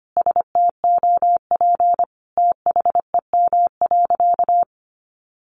Morse code, heard as a keyed tone, about 25 wpm, 700 Hz. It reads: STOP THEM.
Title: Sentences from Top 1100 Words 25wpm